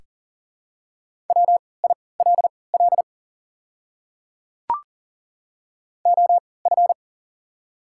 Morse code, heard as a keyed tone, about 40 wpm, spaced as for 23 wpm, 700 Hz. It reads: WILL OF
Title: 3x Character Spacing - Top 100 Words - 40wpm